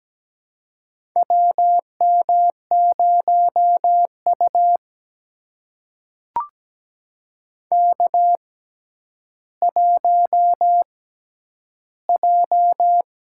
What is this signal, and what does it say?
Morse code, heard as a keyed tone, about 17 wpm, spaced as for 10 wpm, 700 Hz. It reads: WM0U K1J
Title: USA Call Signs - 4 Characters - 6x Character Spacing - 17wpm